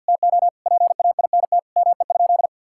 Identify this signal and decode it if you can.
Morse code, heard as a keyed tone, about 47 wpm, 700 Hz. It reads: TO PAINT ME?
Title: Sentences from Top 1000 Words - Review Part 6 - 50wpm